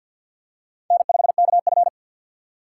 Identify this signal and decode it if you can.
Morse code, heard as a keyed tone, about 52 wpm, 700 Hz. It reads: N 5 K F
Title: USA Call Signs - 6x Character Spacing - 50wpm